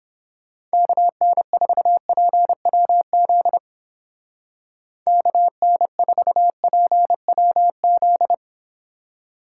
Morse code, heard as a keyed tone, about 29 wpm, 700 Hz. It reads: KN4PWZ KN4PWZ